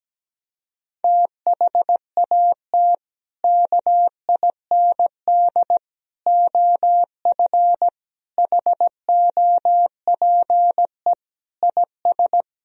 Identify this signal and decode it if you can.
Morse code, heard as a keyed tone, about 17 wpm, 700 Hz. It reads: THAT KIND OF HOPE IS
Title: Sentences from Top 1000 Words - Review Part 3 - 17wpm